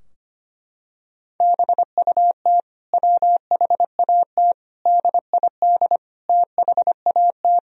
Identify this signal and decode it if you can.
Morse code, heard as a keyed tone, about 25 wpm, 700 Hz. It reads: BUT WHAT DID THAT
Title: Sentences from Top 1000 Words - Review Part 4 - 25wpm